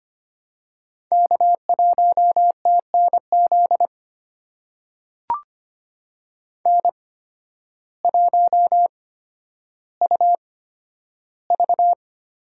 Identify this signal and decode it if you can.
Morse code, heard as a keyed tone, about 25 wpm, spaced as for 12 wpm, 700 Hz. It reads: K1TNZ N1UV
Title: USA Call Signs - 5 Characters - 8x Character Spacing - 25wpm